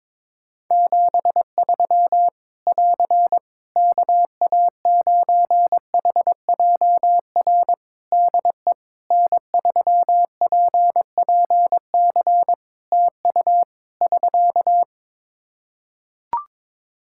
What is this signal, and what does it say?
Morse code, heard as a keyed tone, about 22 wpm, 700 Hz. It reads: 73 <AR> KA9HJR DE N3PPC TU <SK>
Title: Practice QSO - Exchange 4 - Ending - 22wpm